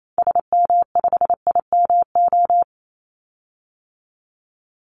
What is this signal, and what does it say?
Morse code, heard as a keyed tone, about 28 wpm, 700 Hz. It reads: SM5IMO
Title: Call Signs Repeated 3x - 28wpm